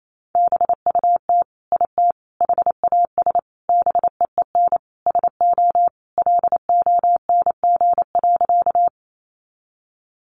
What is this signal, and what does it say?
Morse code, heard as a keyed tone, about 28 wpm, 700 Hz. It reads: BUT IT HAS BEEN SO LONG.
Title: Sentences from Top 100 Words 28wpm